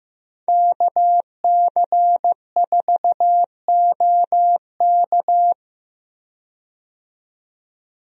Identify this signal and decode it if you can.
Morse code, heard as a keyed tone, about 15 wpm, 700 Hz. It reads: KC4OK
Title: USA Call Signs - 5 Characters - 15wpm